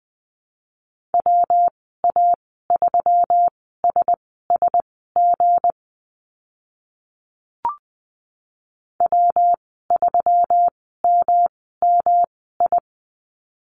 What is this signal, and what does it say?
Morse code, heard as a keyed tone, about 20 wpm, spaced as for 14 wpm, 700 Hz. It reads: WA3SSG W3MMI
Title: 179 Lesson 29g - Slash - Call Signs 20wpm